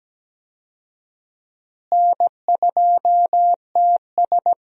Morse code, heard as a keyed tone, about 17 wpm, 700 Hz. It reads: N2TS
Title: USA Call Signs - 4 Characters - Repeated 3x - 17wpm